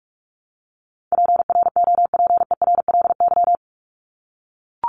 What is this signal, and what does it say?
Morse code, heard as a keyed tone, about 44 wpm, 700 Hz. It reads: PROPERLY
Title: LY Suffix - Top 100 Words with Base Words 45wpm